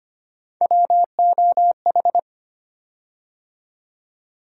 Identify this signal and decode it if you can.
Morse code, heard as a keyed tone, about 25 wpm, 700 Hz. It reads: WOH
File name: Three Letter-Number Combinations - 25wpm